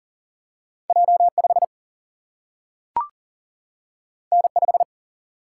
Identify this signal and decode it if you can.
Morse code, heard as a keyed tone, about 40 wpm, 700 Hz. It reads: J5 N5
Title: Single-Letter Single-Number 40wpm